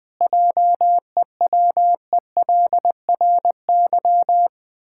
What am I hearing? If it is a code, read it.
Morse code, heard as a keyed tone, about 20 wpm, 700 Hz. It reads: JEWELRY
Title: Pangram Challenge v2 - 20wpm